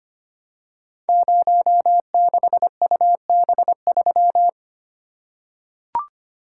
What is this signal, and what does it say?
Morse code, heard as a keyed tone, about 25 wpm, 700 Hz. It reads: 06UB3
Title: Code Groups - 5 Letter:Numbers - 25wpm